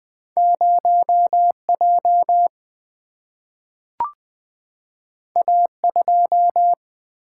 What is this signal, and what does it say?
Morse code, heard as a keyed tone, about 20 wpm, 700 Hz. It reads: 0 J A 2
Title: Single-Letter Single-Number 20wpm